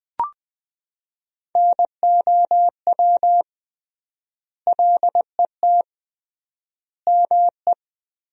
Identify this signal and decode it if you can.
Morse code, heard as a keyed tone, about 20 wpm, 700 Hz. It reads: NOW LET ME
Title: 3x Word-Spacing - Sentences from Top 300 Words - 20wpm